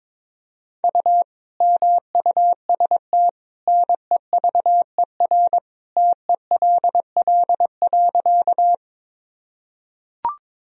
Morse code, heard as a keyed tone, about 22 wpm, 700 Hz. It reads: U MUST NEVER TELL.
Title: Sentences from Top 300 Words and Abbreviations 22wpm